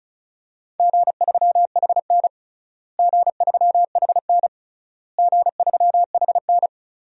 Morse code, heard as a keyed tone, about 36 wpm, 700 Hz. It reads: G3HN G3HN G3HN
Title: DXCC-100 Call Signs - 4 Characters - Repeated 3x - 35wpm